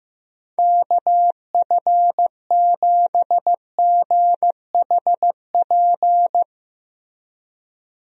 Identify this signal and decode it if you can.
Morse code, heard as a keyed tone, about 15 wpm, 700 Hz. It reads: KF7GHP
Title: USA Call Signs - 6 Characters - Repeated 2x - 15wpm